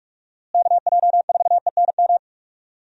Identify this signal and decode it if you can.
Morse code, heard as a keyed tone, about 43 wpm, 700 Hz. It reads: KJ4ENM
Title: USA Call Signs - 6 Characters - Repeated 2x - 45wpm